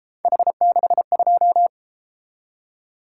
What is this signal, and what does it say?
Morse code, heard as a keyed tone, about 32 wpm, 700 Hz. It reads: H62